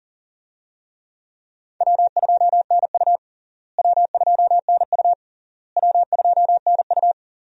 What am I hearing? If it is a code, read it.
Morse code, heard as a keyed tone, about 40 wpm, 700 Hz. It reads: W2NU W2NU W2NU
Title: USA Call Signs - 4 Characters - Repeated 3x - 40wpm